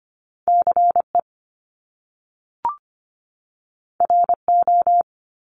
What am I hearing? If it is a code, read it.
Morse code, heard as a keyed tone, about 25 wpm, 700 Hz. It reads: CE RO
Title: Top Two Letter Combinations in Words 25wpm